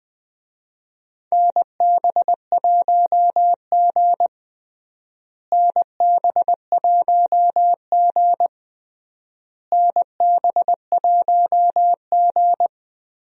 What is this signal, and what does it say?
Morse code, heard as a keyed tone, about 21 wpm, 700 Hz. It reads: NB1G NB1G NB1G
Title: USA Call Signs - 4 Characters - Repeated 3x - 20wpm